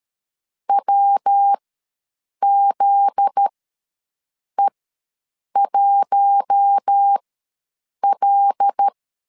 Morse code, heard as a keyed tone, about 13 wpm, spaced as for 7 wpm, 800 Hz. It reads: WZE1L
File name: LEZIONI / 7 GF9Z / 7 GF9Z ws 12wpm 3321